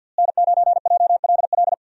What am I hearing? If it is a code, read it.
Morse code, heard as a keyed tone, about 50 wpm, 700 Hz. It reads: N9JLL
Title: USA Call Signs - 5 Characters - Repeated 3x - 50wpm